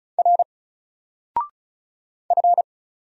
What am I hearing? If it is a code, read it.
Morse code, heard as a keyed tone, about 36 wpm, 700 Hz. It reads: R F
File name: Single-Letter-Number-35wpm